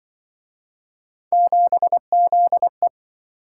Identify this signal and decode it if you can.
Morse code, heard as a keyed tone, about 27 wpm, 700 Hz. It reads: 7ZE